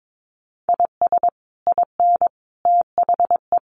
Morse code, heard as a keyed tone, about 22 wpm, 700 Hz. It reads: IS IN THE
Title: Sets of 5 Words Encore2 22wpm